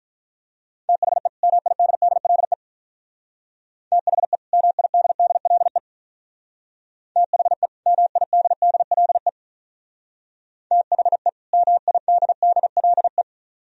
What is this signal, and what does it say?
Morse code, heard as a keyed tone, about 44 wpm, 700 Hz. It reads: THE MIDDLE THE MIDDLE THE MIDDLE THE MIDDLE